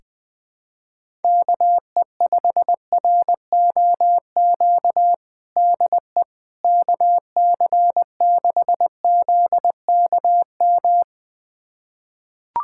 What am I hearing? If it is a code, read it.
Morse code, heard as a keyed tone, about 21 wpm, 700 Hz. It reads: KE5ROQ DE KC6ZKM
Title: USA Callsign DE Callsign - 6 Characters - 20wpm